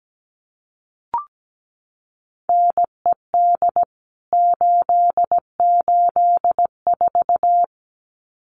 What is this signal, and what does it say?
Morse code, heard as a keyed tone, about 17 wpm, 700 Hz. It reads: NED 884
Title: CWT Exchange 17wpm